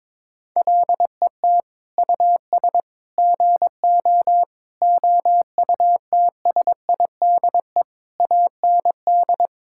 Morse code, heard as a keyed tone, about 22 wpm, 700 Hz. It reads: LET US GO OUTSIDE AND
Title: Sentences from Top 1000 Words - Review Part 6 - 22wpm